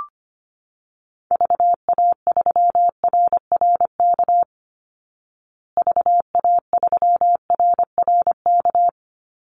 Morse code, heard as a keyed tone, about 25 wpm, 700 Hz. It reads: VA3RRK VA3RRK
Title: Call Signs Repeated 3x - 25wpm